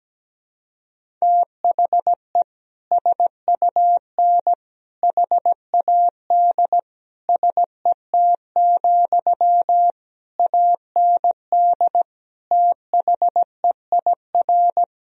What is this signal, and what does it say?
Morse code, heard as a keyed tone, about 17 wpm, 700 Hz. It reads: THE SUN HAD SET, AND THEIR
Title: Sentences from Top 1000 Words - Review Part 1 - 17wpm